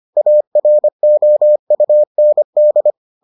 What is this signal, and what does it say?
Morse code, heard as a keyed tone, about 25 wpm, 600 Hz. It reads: AROUND